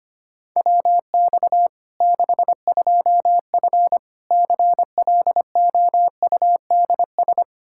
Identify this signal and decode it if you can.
Morse code, heard as a keyed tone, about 25 wpm, 700 Hz. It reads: WX 62F CLOUDS
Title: K1USN SST QSO Practice 25wpm